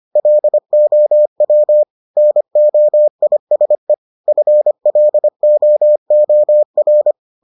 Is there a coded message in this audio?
Yes: Morse code from 0.2 to 7.1 s, heard as a keyed tone, about 25 wpm, 600 Hz. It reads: LOW NOISE FLOOR